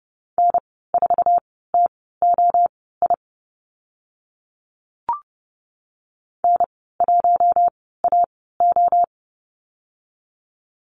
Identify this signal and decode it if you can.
Morse code, heard as a keyed tone, about 30 wpm, spaced as for 17 wpm, 700 Hz. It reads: N4TOI N1AO
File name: Lesson 15g - Number 5 - Call Signs 30wpm